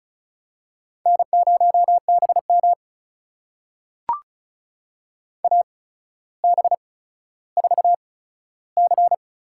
Morse code, heard as a keyed tone, about 36 wpm, spaced as for 15 wpm, 700 Hz. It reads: N0BM AB4C
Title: USA Call Signs - 4 Characters - 8x Character Spacing - 35wpm